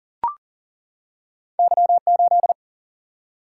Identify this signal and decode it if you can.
Morse code, heard as a keyed tone, about 38 wpm, 700 Hz. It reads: Y8
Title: Single-Letter Single-Number 40wpm